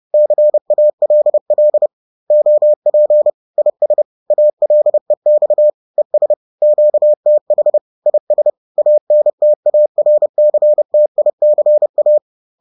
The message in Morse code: CALL OP IS ALEX ES QTH IS ANTARCTICA

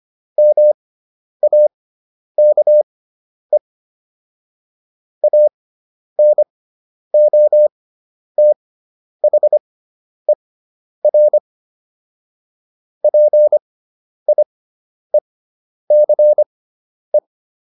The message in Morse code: MAKE ANOTHER PIECE

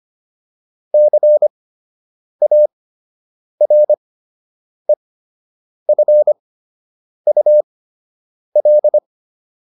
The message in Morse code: CAREFUL

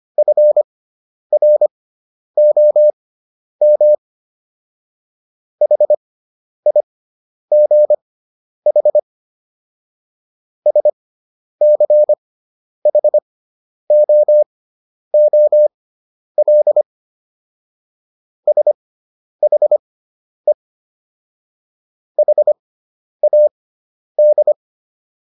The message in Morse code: FROM HIGH SCHOOL SHE HAD

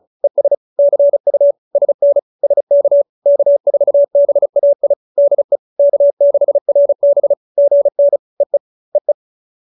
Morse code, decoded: ES CU SN SK K4BAI DE K6RB GN EE EE

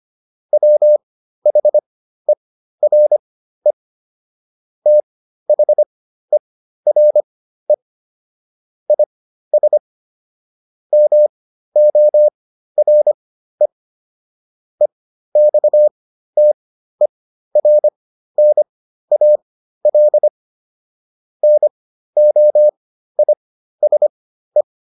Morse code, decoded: WHERE THERE IS MORE EXTERNAL NOISE

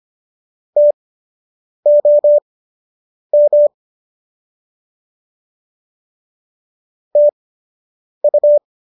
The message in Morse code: TOM TU

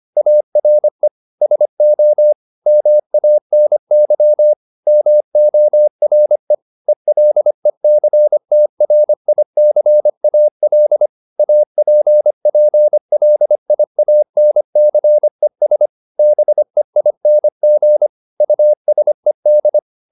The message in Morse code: ARE SO MANY MORE ELECTRICAL APPLIANCES BEING USED

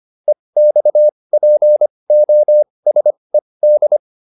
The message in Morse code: EXPOSED

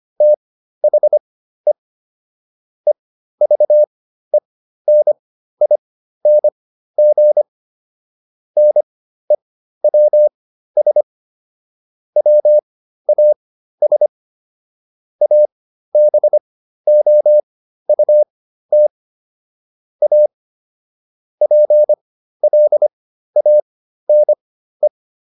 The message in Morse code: THE EVENING NEWS WAS ABOUT A PLANE

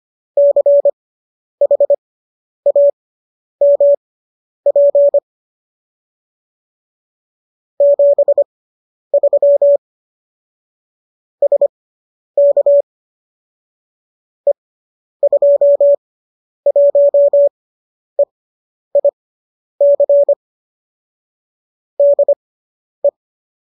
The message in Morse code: CHAMP 73 SK E21EIC DE

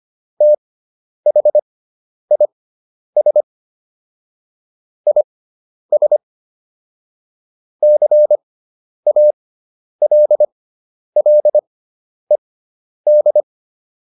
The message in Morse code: THIS IS CALLED

